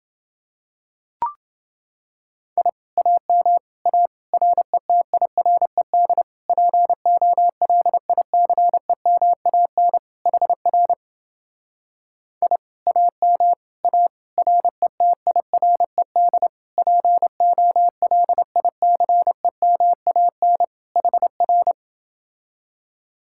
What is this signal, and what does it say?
Morse code, heard as a keyed tone, about 30 wpm, 700 Hz. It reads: I AM A RETIRED POLICEMAN HR I AM A RETIRED POLICEMAN HR